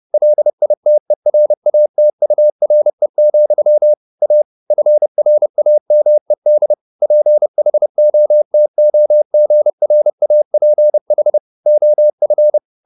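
Morse code, heard as a keyed tone, about 30 wpm, 600 Hz. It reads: LITERATURE, A FRAMED PHOTOGRAPH OF
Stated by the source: Morse code at 30 wpm, the novel Never by Ken Follett in English, from CW med Gnister.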